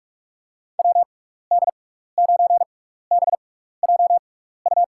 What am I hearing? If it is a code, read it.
Morse code, heard as a keyed tone, about 46 wpm, spaced as for 15 wpm, 700 Hz. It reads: WD9BJU
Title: USA Call Signs - 6 Characters - 6x Character Spacing - 45wpm